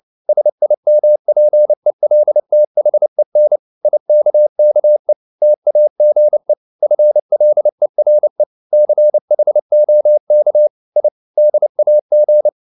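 Morse code, heard as a keyed tone, about 29 wpm, 600 Hz. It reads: SIMPELTHEN IKKE TAGE FLERE CHOK I DAG
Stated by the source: Fast (QRQ) Morse at 29 wpm, from CW med Gnister.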